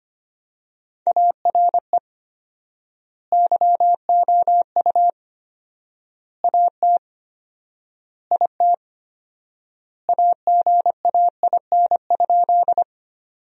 Morse code, heard as a keyed tone, about 25 wpm, 700 Hz. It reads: ARE YOU AT IT AGAIN?
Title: 4x Word-Spacing - Sentences from Top 200 Words - 25wpm